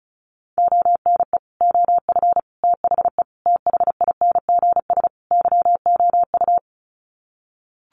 Morse code, heard as a keyed tone, about 34 wpm, 700 Hz. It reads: ONE OF THE THINGS YOU
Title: Sets of 5 Words Encore2 35wpm